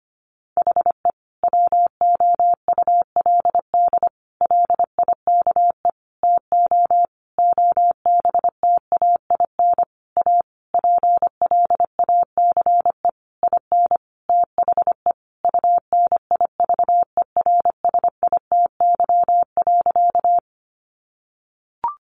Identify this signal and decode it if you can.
Morse code, heard as a keyed tone, about 25 wpm, 700 Hz. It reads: HE WOULD LIKE TO OBTAIN A PLACE IN THE UNIVERSITY.
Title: Sentences from Top 1300 Words 25wpm